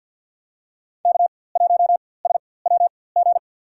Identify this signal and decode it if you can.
Morse code, heard as a keyed tone, about 50 wpm, spaced as for 23 wpm, 700 Hz. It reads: K1SWG
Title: USA Call Signs - 4x Character Spacing - 50wpm